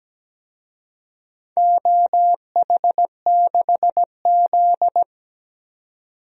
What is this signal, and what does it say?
Morse code, heard as a keyed tone, about 17 wpm, 700 Hz. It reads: OH6Z
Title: DXCC-100 Call Signs - 4 Characters - 17wpm